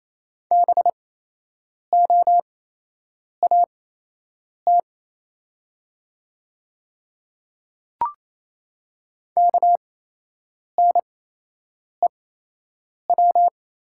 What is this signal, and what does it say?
Morse code, heard as a keyed tone, about 28 wpm, spaced as for 8 wpm, 700 Hz. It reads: BOAT KNEW
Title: 8x Character Spacing - Top 500 Words - 28wpm